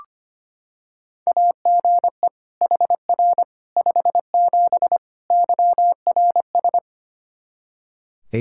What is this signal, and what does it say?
Morse code, heard as a keyed tone, about 25 wpm, 700 Hz. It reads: AGE HR 57 YRS
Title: K1USN SST QSO Practice 25wpm